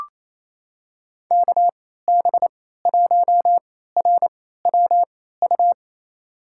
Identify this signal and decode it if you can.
Morse code, heard as a keyed tone, about 28 wpm, spaced as for 16 wpm, 700 Hz. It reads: KB1RWU
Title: USA Call Signs - 6 Characters - 3x Character Spacing - 28wpm